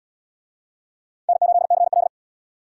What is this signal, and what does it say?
Morse code, heard as a keyed tone, about 67 wpm, 700 Hz. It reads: N8CL